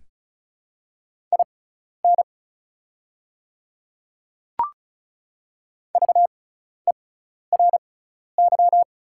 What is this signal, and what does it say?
Morse code, heard as a keyed tone, about 36 wpm, spaced as for 12 wpm, 690 Hz. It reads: IN VERY
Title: 6x Character Spacing - Top 300 Words - 35wpm